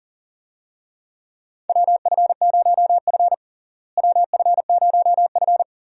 Morse code, heard as a keyed tone, about 40 wpm, 700 Hz. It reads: WF0F WF0F
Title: USA Call Signs - 4 Characters - Repeated 2x - 40wpm